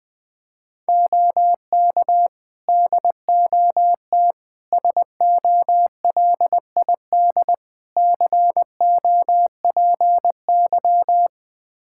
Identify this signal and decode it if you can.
Morse code, heard as a keyed tone, about 20 wpm, 700 Hz. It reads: OK DOT SOLID COPY
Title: Practice QSO - Exchange 3 - More Info - 20wpm